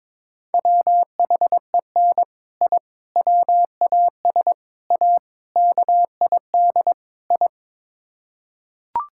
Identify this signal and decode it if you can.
Morse code, heard as a keyed tone, about 21 wpm, 700 Hz. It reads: WHEN I WAS A KID I
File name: practice / 22wpm / Sets of 6 Words - 22wpm